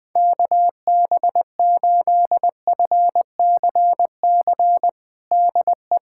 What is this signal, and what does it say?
Morse code, heard as a keyed tone, about 20 wpm, 700 Hz. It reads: KB8FCC DE
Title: USA Callsign DE Callsign 20wpm